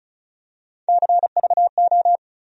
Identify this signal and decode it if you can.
Morse code, heard as a keyed tone, about 36 wpm, 700 Hz. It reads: CVO
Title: USA Call Sign Postfixes - 3 Letters 35wpm